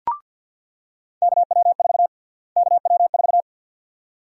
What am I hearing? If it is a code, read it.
Morse code, heard as a keyed tone, about 50 wpm, 700 Hz. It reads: KW4 KW4
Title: USA Call Sign Prefixes - Repeated 2x - 50wpm